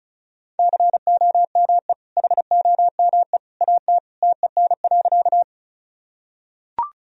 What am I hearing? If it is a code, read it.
Morse code, heard as a keyed tone, about 34 wpm, 700 Hz. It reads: COME HOME AT TEN.
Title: Sentences from Top 1000 Words - Review Part 2 - 35wpm